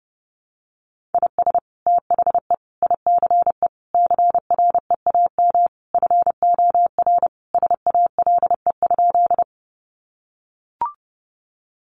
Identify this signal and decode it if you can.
Morse code, heard as a keyed tone, about 30 wpm, 700 Hz. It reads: IS THE ICE CREAM FOR SALE?
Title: Sentences from Top 1500 Words 30wpm